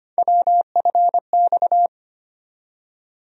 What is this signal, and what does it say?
Morse code, heard as a keyed tone, about 25 wpm, 700 Hz. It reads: WFX
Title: USA Call Sign Postfixes - 3 Letters 25wpm